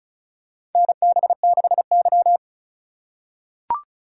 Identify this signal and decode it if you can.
Morse code, heard as a keyed tone, about 35 wpm, 700 Hz. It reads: NB6Y
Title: USA Call Signs - 4 Characters - 3x Character Spacing - 35wpm